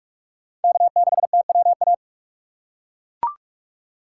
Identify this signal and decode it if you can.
Morse code, heard as a keyed tone, about 43 wpm, 700 Hz. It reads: K6TWA
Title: USA Call Signs - 2x Character Spacing - 45wpm